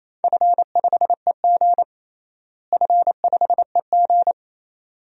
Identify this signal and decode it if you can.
Morse code, heard as a keyed tone, about 28 wpm, 700 Hz. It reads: F5EG F5EG
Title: DXCC-100 Call Signs - 4 Characters - Repeated 2x - 28wpm